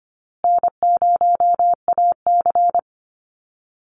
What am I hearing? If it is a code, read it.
Morse code, heard as a keyed tone, about 25 wpm, 700 Hz. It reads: N0AC
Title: Easy Call Signs Repeated 3x - 25wpm